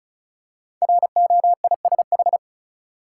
Morse code, heard as a keyed tone, about 36 wpm, 700 Hz. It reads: ROISH
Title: Code Groups - 5 Letters - 35wpm